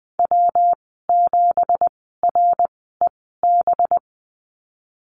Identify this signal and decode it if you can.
Morse code, heard as a keyed tone, about 20 wpm, spaced as for 15 wpm, 700 Hz. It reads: W7REB
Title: 193 Lesson 31g - Letter v - Call Signs 20wpm